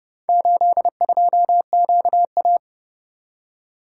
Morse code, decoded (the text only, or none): 82QA